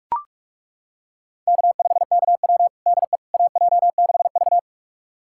Morse code, heard as a keyed tone, about 41 wpm, 700 Hz. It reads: K5KW DE AJ6V
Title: USA Callsign DE Callsign - 4 Characters - 45wpm